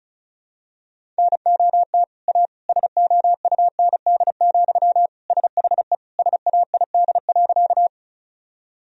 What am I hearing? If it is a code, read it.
Morse code, heard as a keyed tone, about 36 wpm, 700 Hz. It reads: NOT A SOUND, SHE SAID.
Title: Sentences from Top 1000 Words - Review Part 1 - 35wpm